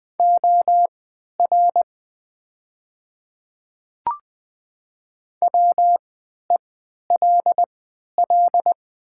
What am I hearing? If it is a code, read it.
Morse code, heard as a keyed tone, about 20 wpm, spaced as for 11 wpm, 700 Hz. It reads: OR WELL
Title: 3x Character Spacing - Top 200 Words - 20wpm